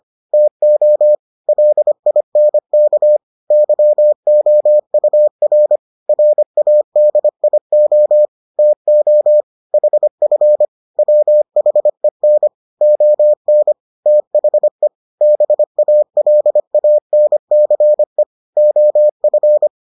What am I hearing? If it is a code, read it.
Morse code, heard as a keyed tone, about 25 wpm, 600 Hz. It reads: TO LINK YOUR RADIO TO HF WHEN ON THE BALANCE OF